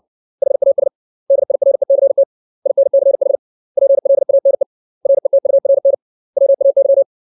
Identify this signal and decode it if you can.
Morse code, heard as a keyed tone, about 60 wpm, 550 Hz. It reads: HAS DIRECT INCH PLANE RETURN WAY
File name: 632 words / 632 Words 60WPM